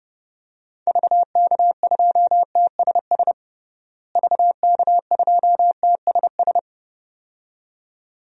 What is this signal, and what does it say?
Morse code, heard as a keyed tone, about 30 wpm, 700 Hz. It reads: VK2TSS VK2TSS
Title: DXCC-100 Call Signs - 6 Characters - Repeated 2x - 30wpm